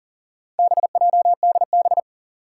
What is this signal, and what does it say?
Morse code, heard as a keyed tone, about 37 wpm, 700 Hz. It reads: BJDB
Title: Four Challenging Letter-Number Combinations 40wpm